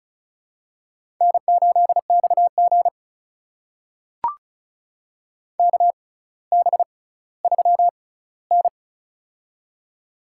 Morse code, heard as a keyed tone, about 35 wpm, spaced as for 18 wpm, 700 Hz. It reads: N8XG KB3N
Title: USA Call Signs - 4 Characters - 6x Character Spacing - 35wpm